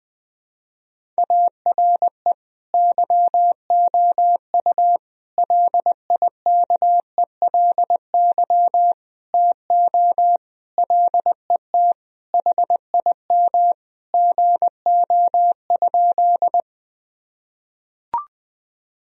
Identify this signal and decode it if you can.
Morse code, heard as a keyed tone, about 20 wpm, 700 Hz. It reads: ARE YOU LIKELY TO LET HIM GO?
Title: Sentences from Top 1000 Words - Review Part 4 - 20wpm